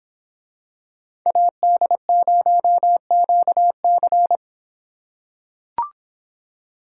Morse code, heard as a keyed tone, about 23 wpm, 700 Hz. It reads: AD0QC